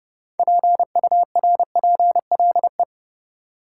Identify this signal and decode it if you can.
Morse code, heard as a keyed tone, about 32 wpm, 700 Hz. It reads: PURPLE
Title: Six Letter Words - 30wpm